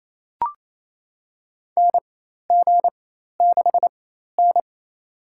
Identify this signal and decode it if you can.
Morse code, heard as a keyed tone, about 28 wpm, spaced as for 13 wpm, 700 Hz. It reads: NG6N
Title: USA Call Signs - 4 Characters - 4x Character Spacing - 28wpm